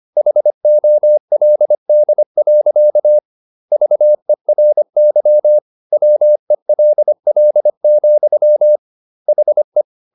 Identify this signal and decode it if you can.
Morse code, heard as a keyed tone, about 25 wpm, 600 Hz. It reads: HOLD. VERY WELL, HE
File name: Never 0025wpm